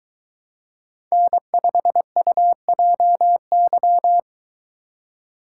Morse code, heard as a keyed tone, about 22 wpm, 700 Hz. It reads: N5UJY